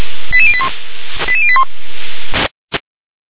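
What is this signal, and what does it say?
Five-tone selective call (ZVEI), heard as quick runs of five tones, tones of about 70 ms. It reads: 88081 88081
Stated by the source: Received on an SDR.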